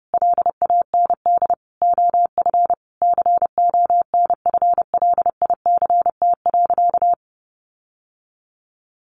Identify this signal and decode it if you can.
Morse code, heard as a keyed tone, about 30 wpm, 700 Hz. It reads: LAND OF CONFLICT.
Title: Sentences from Top 1300 Words 30wpm